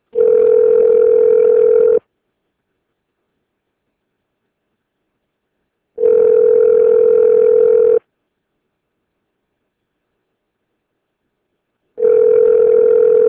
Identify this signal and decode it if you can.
Ringback tone (North America), 440 + 480 Hz.